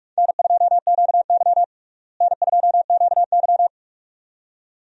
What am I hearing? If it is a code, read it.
Morse code, heard as a keyed tone, about 44 wpm, 700 Hz. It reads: N2QY N2QY
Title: USA Call Signs - 4 Characters - Repeated 2x - 45wpm